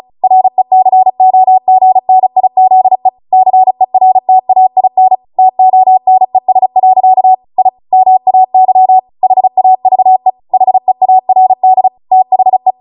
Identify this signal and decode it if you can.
Morse code, heard as a keyed tone, about 36 wpm, 750 Hz. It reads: RECOGNIZE CERTAIN TONES. I MAY HAVE HEARD THE